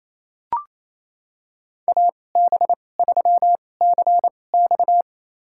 Morse code, heard as a keyed tone, about 28 wpm, spaced as for 20 wpm, 700 Hz. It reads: AB3CX
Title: USA Call Signs - 5 Characters - 2x Character Spacing - 28wpm